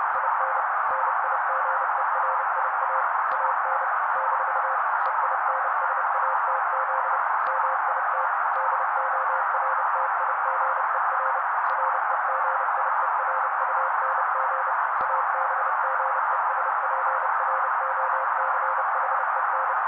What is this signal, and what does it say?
Morse code, heard as a keyed tone, about 30 wpm, 530 Hz. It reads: IN NIGERIA AN - EINSATZ MIT DORTIGER REGIERUNG ABGESPROCHEN
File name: daily / 29wpm noise